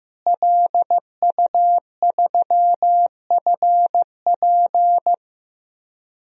Morse code, heard as a keyed tone, about 15 wpm, 700 Hz. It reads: LU3FP
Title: DXCC-100 Call Signs - 5 Characters - Repeated 2x - 15wpm